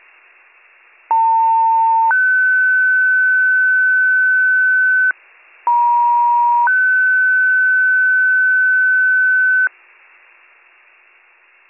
Two-tone page, heard as a one-second tone, then a three-second tone, about 900 then 1500 Hz, about 950 then 1500 Hz. It is heard from 1.1 to 9.7 s.